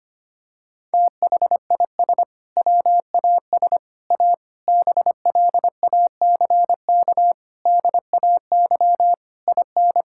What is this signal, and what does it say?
Morse code, heard as a keyed tone, about 25 wpm, 700 Hz. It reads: THIS WAS A BLACK DAY IN
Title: Sentences from Top 1600 Words 25wpm